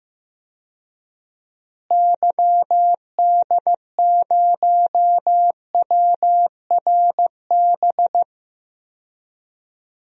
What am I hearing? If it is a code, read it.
Morse code, heard as a keyed tone, about 15 wpm, 700 Hz. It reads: YD0WRB
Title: DXCC-100 Call Signs - 6 Characters - Repeated 3x - 15wpm